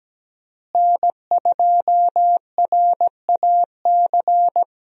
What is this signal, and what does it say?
Morse code, heard as a keyed tone, about 17 wpm, 700 Hz. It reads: N2RAC
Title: USA Call Signs - 5 Characters - Repeated 3x - 17wpm